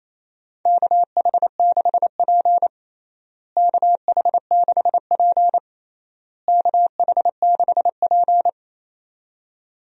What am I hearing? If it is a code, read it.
Morse code, heard as a keyed tone, about 28 wpm, 700 Hz. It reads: KH6P KH6P KH6P
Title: DXCC-100 Call Signs - 4 Characters - Repeated 3x - 28wpm